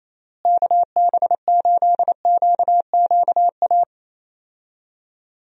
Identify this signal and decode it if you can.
Morse code, heard as a keyed tone, about 28 wpm, 700 Hz. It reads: KB8QQA
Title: USA Call Signs - 6 Characters - Repeated 2x - 28wpm